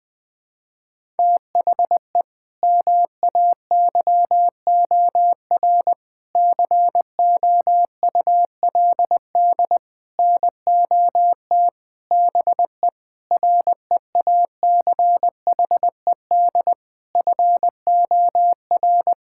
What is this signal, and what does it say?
Morse code, heard as a keyed tone, about 21 wpm, 700 Hz. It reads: THE MAYOR COULD NOT BE REACHED FOR
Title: Sentences with 6-Word Phrases - 20wpm